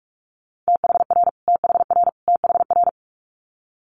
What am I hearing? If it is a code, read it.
Morse code, heard as a keyed tone, about 46 wpm, 700 Hz. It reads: THR THR THR
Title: Multi-letter Phonemes Repeated 3x 45wpm